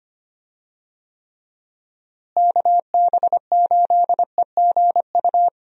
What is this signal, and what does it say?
Morse code, heard as a keyed tone, about 25 wpm, 700 Hz. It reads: KB8EGU
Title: USA Call Signs - 6 Characters - Repeated 3x - 25wpm